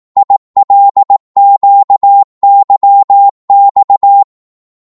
Morse code, heard as a keyed tone, about 18 wpm, 800 Hz. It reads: ILQYX
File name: Morse / ltr-18wpm